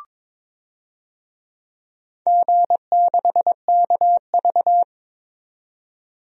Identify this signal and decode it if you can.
Morse code, heard as a keyed tone, about 22 wpm, 700 Hz. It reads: G6KV
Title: DXCC-100 Call Signs - 4 Characters - 22wpm